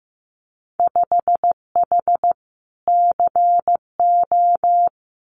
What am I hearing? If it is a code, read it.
Morse code, heard as a keyed tone, about 15 wpm, 700 Hz. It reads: 5H CO
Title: Winter Field Day Exchanges Encore 15wpm